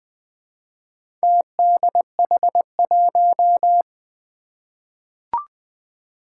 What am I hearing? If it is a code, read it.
Morse code, heard as a keyed tone, about 20 wpm, 700 Hz. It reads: TDH1
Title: Four Challenging Letter-Number Combinations 20wpm